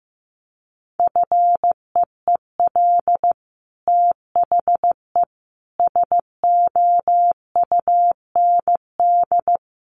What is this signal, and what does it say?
Morse code, heard as a keyed tone, about 15 wpm, 700 Hz. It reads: FEEL THE SOUND
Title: Sentences from Top 1400 Words 15wpm